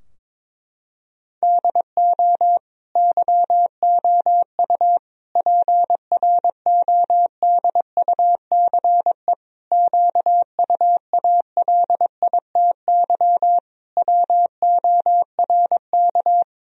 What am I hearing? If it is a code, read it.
Morse code, heard as a keyed tone, about 21 wpm, 700 Hz. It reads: DO YOU PRODUCE QUALITY WORK
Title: Sentences from Top 1000 Words - Review Part 4 - 22wpm